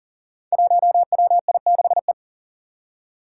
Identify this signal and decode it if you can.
Morse code, heard as a keyed tone, about 41 wpm, 700 Hz. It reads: 1WI6E
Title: Five Challenging Letter-Number Combinations 40wpm